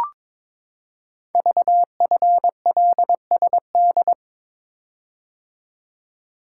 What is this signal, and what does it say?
Morse code, heard as a keyed tone, about 22 wpm, 700 Hz. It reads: VFLSD
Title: Code Groups - 5 Letters - 22wpm